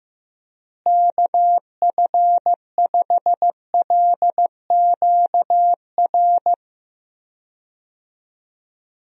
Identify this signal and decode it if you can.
Morse code, heard as a keyed tone, about 16 wpm, 700 Hz. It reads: KF5LQR
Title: USA Call Signs - 6 Characters - Repeated 3x - 15wpm